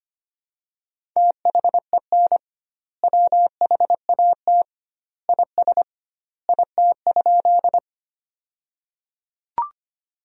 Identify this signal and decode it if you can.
Morse code, heard as a keyed tone, about 25 wpm, 700 Hz. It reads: THEN WHAT IS IT?
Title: 2x Word-Spacing - Sentences from Top 100 Words - 25wpm